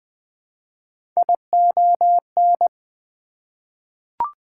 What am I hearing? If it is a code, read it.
Morse code, heard as a keyed tone, about 20 wpm, 700 Hz. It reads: ION
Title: Prefixes and Suffixes 20wpm